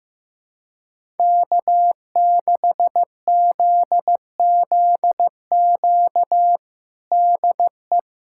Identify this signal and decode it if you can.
Morse code, heard as a keyed tone, about 16 wpm, 700 Hz. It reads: K6ZZQ DE
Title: USA Callsign DE Callsign 15wpm